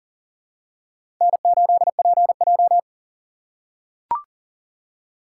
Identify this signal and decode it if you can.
Morse code, heard as a keyed tone, about 40 wpm, 700 Hz. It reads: N8PJ
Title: USA Call Signs - 4 Characters - 2x Character Spacing - 40wpm